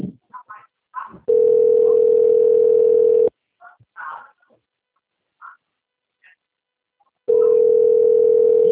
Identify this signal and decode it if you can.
Ringback tone (North America), 440 + 480 Hz.
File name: out- - - - - .242